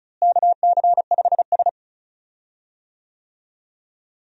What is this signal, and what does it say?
Morse code, heard as a keyed tone, about 36 wpm, 700 Hz. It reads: KC5S
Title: USA Call Signs - 4 Characters - Repeated 2x - 35wpm